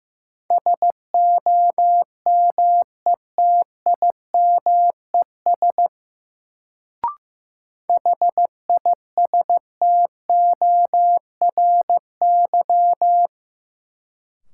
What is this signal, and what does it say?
Morse code, heard as a keyed tone, about 15 wpm, 700 Hz. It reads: SOMETIMES HISTORY
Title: Sentences from Top 600 Words - New Word Drill - 15wpm